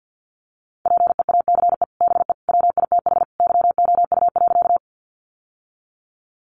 Morse code, heard as a keyed tone, about 50 wpm, 700 Hz. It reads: PEACE BE WITH YOU.
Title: Sentences from Top 1500 Words 50wpm